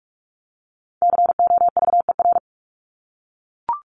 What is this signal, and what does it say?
Morse code, heard as a keyed tone, about 44 wpm, 700 Hz. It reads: COVER
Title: Top 300 Words Repeated 45wpm